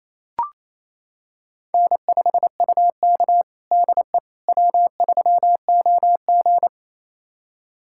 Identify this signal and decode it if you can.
Morse code, heard as a keyed tone, about 28 wpm, 700 Hz. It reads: N5UK DE W3OG
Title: USA Callsign DE Callsign - 4 Characters - 28wpm